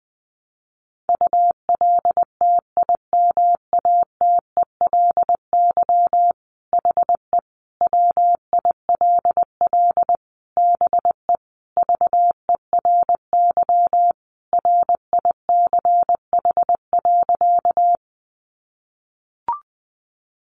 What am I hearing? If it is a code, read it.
Morse code, heard as a keyed tone, about 20 wpm, 700 Hz. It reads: ULTIMATELY HE WILL BE VERY RICH.
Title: Sentences from Top 1200 Words 20wpm